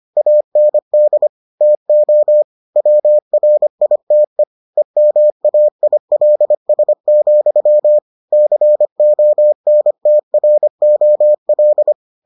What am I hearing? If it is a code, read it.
Morse code, heard as a keyed tone, about 25 wpm, 600 Hz. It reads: AND TO WRITE EMAILS, CONTROL